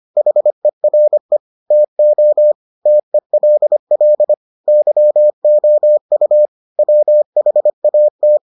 Morse code, heard as a keyed tone, about 25 wpm, 600 Hz. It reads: HERE TO TELL YOU WHAT